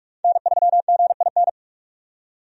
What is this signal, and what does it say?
Morse code, heard as a keyed tone, about 46 wpm, 700 Hz. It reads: N3GIN
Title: USA Call Signs - 5 Characters - 45wpm